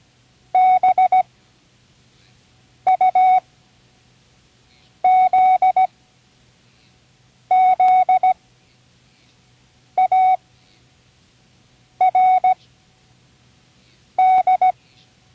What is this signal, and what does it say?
Morse code, heard as a keyed tone, about 17 wpm, spaced as for 5 wpm, 740 Hz. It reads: BUZZARD
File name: CWResources / Learning Morse Code (from Tune in the World) / 29 words using all letters